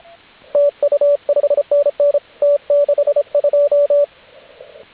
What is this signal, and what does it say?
Morse code, heard as a keyed tone, about 26 wpm, 580 Hz, with radio noise. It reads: TU 5NN T62
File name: AUDIO2019 / HG8DX CW